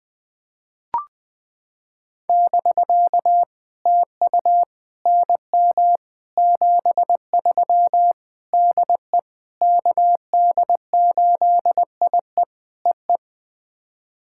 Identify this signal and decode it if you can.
Morse code, heard as a keyed tone, about 20 wpm, 700 Hz. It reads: <BK> TU NM 73 DE KD8IE EE
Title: POTA QSO Practice - 20wpm